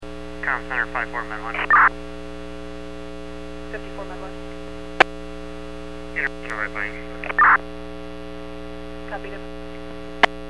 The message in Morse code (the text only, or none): none